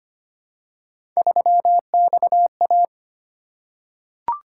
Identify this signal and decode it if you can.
Morse code, heard as a keyed tone, about 25 wpm, 700 Hz. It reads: 3XA
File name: Three Letter-Number Combinations - 25wpm